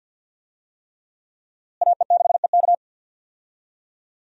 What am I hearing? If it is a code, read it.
Morse code, heard as a keyed tone, about 52 wpm, 700 Hz. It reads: AE6EK
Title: USA Call Signs - 5 Characters - Repeated 3x - 50wpm